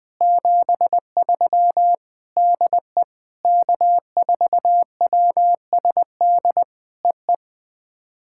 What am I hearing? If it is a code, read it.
Morse code, heard as a keyed tone, about 20 wpm, 700 Hz. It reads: 73 DE K4WSD EE
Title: POTA QSO Practice - 20wpm